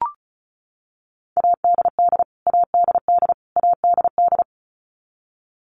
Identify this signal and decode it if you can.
Morse code, heard as a keyed tone, about 34 wpm, 700 Hz. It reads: ADD ADD ADD
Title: Top 300 Words Repeated 35wpm